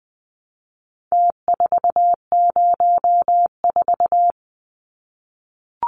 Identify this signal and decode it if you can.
Morse code, heard as a keyed tone, about 20 wpm, 700 Hz. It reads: T404
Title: Numbers - Contest Style - 4-Digit 20wpm